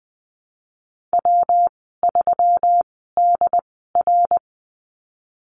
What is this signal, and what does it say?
Morse code, heard as a keyed tone, about 20 wpm, spaced as for 15 wpm, 700 Hz. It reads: W3DR
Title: 140 Lesson 23g - Letter f - Call Signs 20wpm